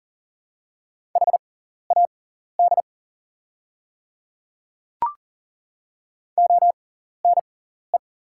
Morse code, heard as a keyed tone, about 41 wpm, spaced as for 14 wpm, 700 Hz. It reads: HAD ONE
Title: 6x Character Spacing - Top 100 Words - 40wpm